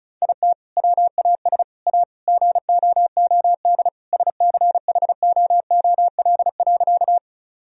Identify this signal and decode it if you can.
Morse code, heard as a keyed tone, about 36 wpm, 700 Hz. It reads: IT WAS A GOOD SCHOOL.
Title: Sentences from Top 1000 Words - Review Part 1 - 35wpm